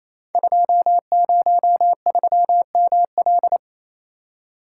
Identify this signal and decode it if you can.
Morse code, heard as a keyed tone, about 27 wpm, 700 Hz. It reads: 203ML
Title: Code Groups - 5 Letter:Numbers - 28wpm